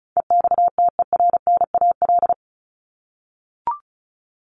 Morse code, heard as a keyed tone, about 34 wpm, 700 Hz. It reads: EXTERNAL
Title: Top 1751-2000 Words 35wpm